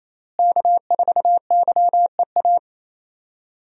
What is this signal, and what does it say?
Morse code, heard as a keyed tone, about 27 wpm, 700 Hz. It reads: K4YEA
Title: USA Call Signs - 5 Characters - Repeated 3x - 28wpm